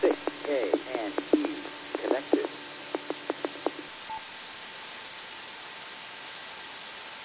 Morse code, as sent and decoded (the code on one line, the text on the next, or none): none